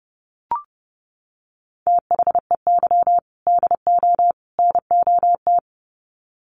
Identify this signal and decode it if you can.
Morse code, heard as a keyed tone, about 30 wpm, 700 Hz. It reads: THEY DO NOT
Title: Sets of 3 Words 30wpm